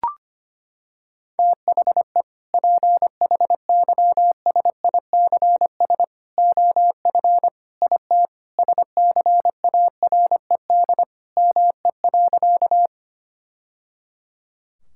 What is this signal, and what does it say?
Morse code, heard as a keyed tone, about 25 wpm, 700 Hz. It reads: THE PHYSICS OF IT SCARED ME.
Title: Sentences from Top 1000 Words - Review Part 5 - 25wpm